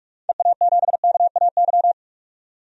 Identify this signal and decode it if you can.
Morse code, heard as a keyed tone, about 47 wpm, 700 Hz. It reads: EA7KAY
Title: DXCC-100 Call Signs - 6 Characters - 45wpm